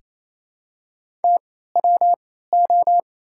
Morse code, heard as a keyed tone, about 28 wpm, spaced as for 16 wpm, 700 Hz. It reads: TWO
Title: 3x Character Spacing - Top 100 Words - 28wpm